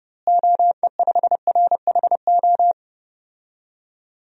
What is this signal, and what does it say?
Morse code, heard as a keyed tone, about 30 wpm, 700 Hz. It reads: OE5RHO
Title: DXCC-100 Call Signs - 6 Characters - Repeated 3x - 30wpm